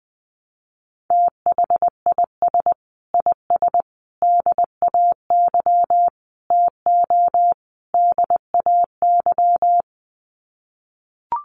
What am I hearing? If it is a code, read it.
Morse code, heard as a keyed tone, about 20 wpm, 700 Hz. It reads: THIS IS DAY TO DAY
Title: Sets of 5 Words Encore 20wpm